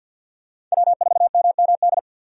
Morse code, heard as a keyed tone, about 50 wpm, 700 Hz. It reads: W4MMD
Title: K1USN SST QSO Practice - 50wpm